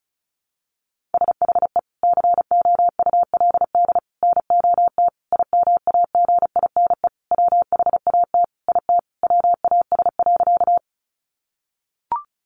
Morse code, heard as a keyed tone, about 34 wpm, 700 Hz. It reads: SHE COULD NOT IMAGINE WHAT IT WAS.
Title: Sentences from Top 1400 Words 35wpm